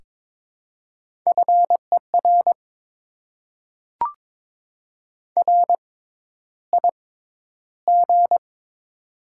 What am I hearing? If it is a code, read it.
Morse code, heard as a keyed tone, about 22 wpm, spaced as for 11 wpm, 700 Hz. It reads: FER RIG
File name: Top 20 QSO Elements - 6x Character Spacing - 22wpm